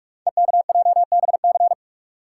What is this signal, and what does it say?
Morse code, heard as a keyed tone, about 44 wpm, 690 Hz. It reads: EKJBC
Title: Code Groups - 5 Letters - 45wpm